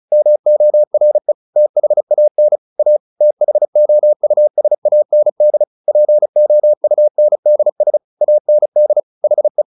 Morse code, heard as a keyed tone, about 34 wpm, 600 Hz. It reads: MORE THAN A THOUSAND POUNDS AND HE
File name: the old man and the sea / the old man and the sea 1 35wpm